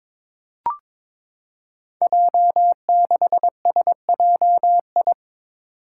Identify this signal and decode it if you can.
Morse code, heard as a keyed tone, about 22 wpm, 700 Hz. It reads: J6SJI
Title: Five Challenging Letter-Number Combinations 22wpm